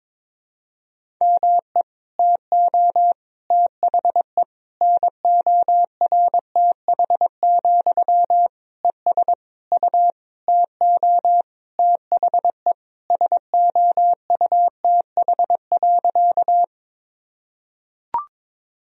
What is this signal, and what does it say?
Morse code, heard as a keyed tone, about 21 wpm, 700 Hz. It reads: ME TO THE NORTH, ES U TO THE SOUTH.
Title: Sentences from Top 400 Words and Abbreviations 22wpm